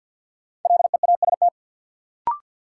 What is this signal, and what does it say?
Morse code, heard as a keyed tone, about 52 wpm, 700 Hz. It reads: LEAST
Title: Sentences from Top 700 Words - New Word Drill - 50wpm